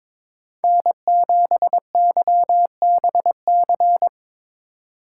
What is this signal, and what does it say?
Morse code, heard as a keyed tone, about 22 wpm, 700 Hz. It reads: N7YBC
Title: USA Call Signs - 5 Characters - 3x Character Spacing - 22wpm